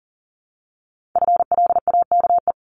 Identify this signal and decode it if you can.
Morse code, heard as a keyed tone, about 42 wpm, 700 Hz. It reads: FLAKE
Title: Multi-letter Phonemes 40wpm